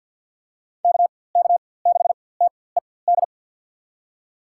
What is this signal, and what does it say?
Morse code, heard as a keyed tone, about 53 wpm, spaced as for 24 wpm, 700 Hz. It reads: KK6TED
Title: USA Call Signs - 4x Character Spacing - 50wpm